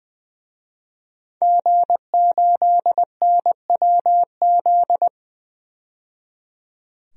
Morse code, heard as a keyed tone, about 20 wpm, 700 Hz. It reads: G8NWZ
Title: DXCC-100 Call Signs - 5 Characters - 20wpm